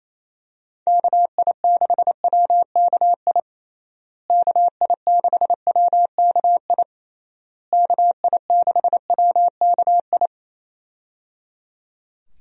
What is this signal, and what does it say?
Morse code, heard as a keyed tone, about 28 wpm, 700 Hz. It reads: KI6WKI KI6WKI KI6WKI
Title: USA Call Signs - 6 Characters - Repeated 3x - 28wpm